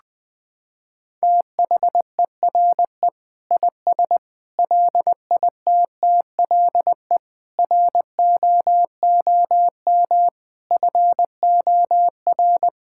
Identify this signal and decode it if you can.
Morse code, heard as a keyed tone, about 20 wpm, 700 Hz. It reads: THERE IS LITTLE ROOM FOR
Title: Sentences from Top 1000 Words - Review Part 6 - 20wpm